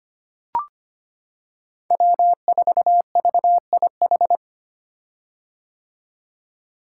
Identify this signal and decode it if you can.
Morse code, heard as a keyed tone, about 25 wpm, 700 Hz. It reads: W4VIH
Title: Code Groups - 5 Letter:Numbers - 25wpm